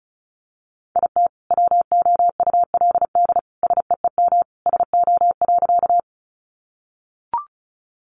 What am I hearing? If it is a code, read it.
Morse code, heard as a keyed tone, about 34 wpm, 700 Hz. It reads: IT WOULD SEEM SO.
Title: Sentences from Top 300 Words 35wpm